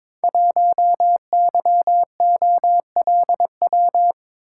Morse code, heard as a keyed tone, about 22 wpm, 700 Hz. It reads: 1YOLW
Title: Code Groups - 5 Letter:Numbers - 22wpm